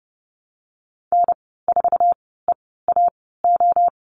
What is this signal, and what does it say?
Morse code, heard as a keyed tone, about 30 wpm, spaced as for 17 wpm, 700 Hz. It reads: N4EAO
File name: Lesson 15g - Number 5 - Call Signs 30wpm